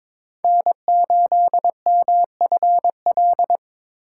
Morse code, heard as a keyed tone, about 22 wpm, 700 Hz. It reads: N8MFL
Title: USA Call Signs - 3x Character Spacing - 22wpm